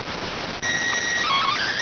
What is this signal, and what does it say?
SSTV (slow-scan television), PD 290.